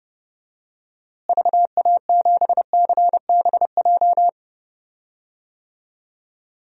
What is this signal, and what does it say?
Morse code, heard as a keyed tone, about 30 wpm, 700 Hz. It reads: VA7CBJ
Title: DXCC-100 Call Signs - 6 Characters - Repeated 3x - 30wpm